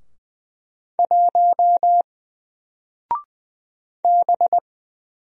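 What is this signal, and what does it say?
Morse code, heard as a keyed tone, about 20 wpm, 700 Hz. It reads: 1 B
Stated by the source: Single letter and number Morse drill at 20 wpm, from N4LCD.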